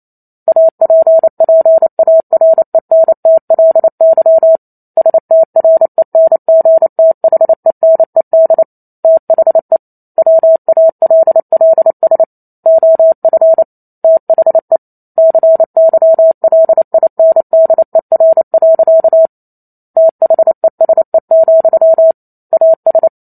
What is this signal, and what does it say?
Morse code, heard as a keyed tone, about 28 wpm, 650 Hz. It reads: APPARENTLY STRENGTHENED THE WALLS OF THE CYLINDER. THESE, AS